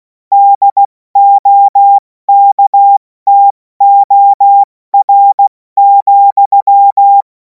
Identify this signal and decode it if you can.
Morse code, heard as a keyed tone, about 16 wpm, 800 Hz. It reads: DOKTOR,
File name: Morsekurs / Witze / Witz 07. Juli 2026 70 F80